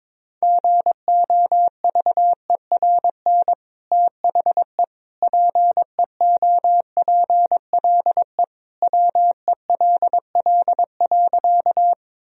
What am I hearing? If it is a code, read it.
Morse code, heard as a keyed tone, about 21 wpm, 700 Hz. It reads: GOVERN THE PEOPLE WELL.
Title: Sentences from Top 1000 Words - Review Part 2 - 22wpm